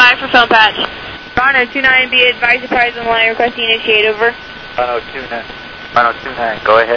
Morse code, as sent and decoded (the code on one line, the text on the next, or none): none